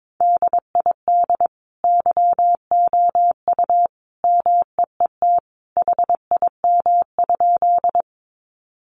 Morse code, heard as a keyed tone, about 22 wpm, 700 Hz. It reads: DID YOU MEET HIM?
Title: Sentences from Top 1100 Words 22wpm